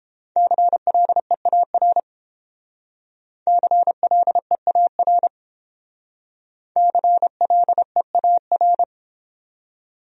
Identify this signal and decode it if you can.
Morse code, heard as a keyed tone, about 31 wpm, 700 Hz. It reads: CLEAR CLEAR CLEAR